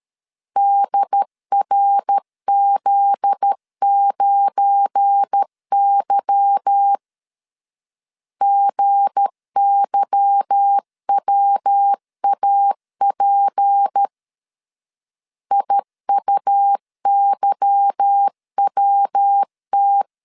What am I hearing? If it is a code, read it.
Morse code, heard as a keyed tone, about 13 wpm, 800 Hz. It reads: DRZ9Y GYWAP IUYWT
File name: LEZIONI / 8 YJ45 / 8 Y 12wpm 2228